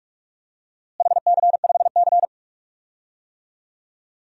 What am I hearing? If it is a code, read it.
Morse code, heard as a keyed tone, about 46 wpm, 700 Hz. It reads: HC5C
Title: DXCC-100 Call Signs - 4 Characters - Repeated 3x - 45wpm